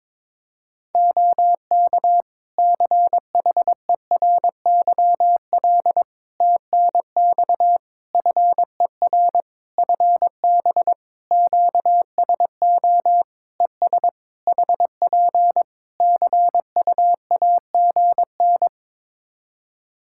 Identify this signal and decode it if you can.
Morse code, heard as a keyed tone, about 22 wpm, 700 Hz. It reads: OK CHERYL TNX FER FB QSO ES HP CUAGN
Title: Practice QSO - Exchange 4 - Ending - 22wpm